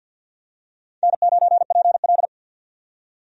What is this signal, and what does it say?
Morse code, heard as a keyed tone, about 52 wpm, 700 Hz. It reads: N9PL
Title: USA Call Signs - 4 Characters - 50wpm